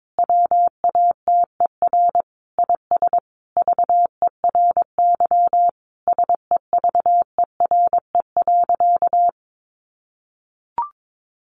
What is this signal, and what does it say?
Morse code, heard as a keyed tone, about 22 wpm, 700 Hz. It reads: WATER IS VERY SEVERE.
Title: Sentences from Top 1500 Words 22wpm